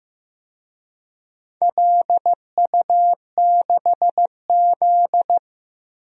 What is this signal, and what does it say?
Morse code, heard as a keyed tone, about 15 wpm, 700 Hz. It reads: LU6Z
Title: DXCC-100 Call Signs - 4 Characters - Repeated 2x - 15wpm